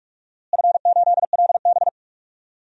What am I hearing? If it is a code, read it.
Morse code, heard as a keyed tone, about 46 wpm, 700 Hz. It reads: F8LB
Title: DXCC-100 Call Signs - 4 Characters - Repeated 3x - 45wpm